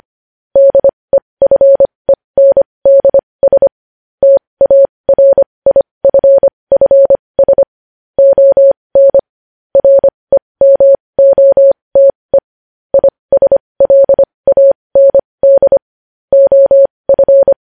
Morse code, heard as a keyed tone, about 25 wpm, spaced as for 20 wpm, 550 Hz. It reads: DEFENDS TARIFFS ON REMOTE ISLAND OF
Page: news headlines in morse code at 20 wpm